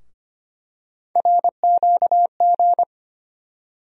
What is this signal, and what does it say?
Morse code, heard as a keyed tone, about 25 wpm, 700 Hz. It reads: RQG
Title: USA Call Sign Postfixes - 3 Letters 25wpm